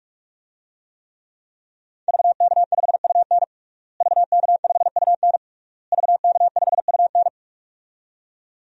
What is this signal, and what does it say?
Morse code, heard as a keyed tone, about 41 wpm, 700 Hz. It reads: VK5UN VK5UN VK5UN
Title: DXCC-100 Call Signs - 5 Characters - Repeated 3x - 45wpm